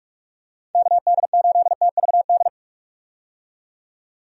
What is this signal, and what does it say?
Morse code, heard as a keyed tone, about 44 wpm, 700 Hz. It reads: KD8TVD
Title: USA Call Signs - 6 Characters - 45wpm